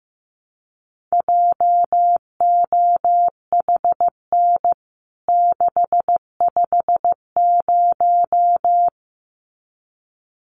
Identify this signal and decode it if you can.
Morse code, heard as a keyed tone, about 14 wpm, 700 Hz. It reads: JOHN 650
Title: CWT Exchange 15wpm